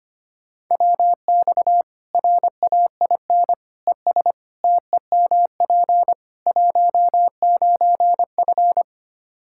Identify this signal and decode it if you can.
Morse code, heard as a keyed tone, about 25 wpm, 700 Hz. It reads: WX RAIN ES TEMP 19F
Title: Practice QSO - Exchange 2 - Info - 25wpm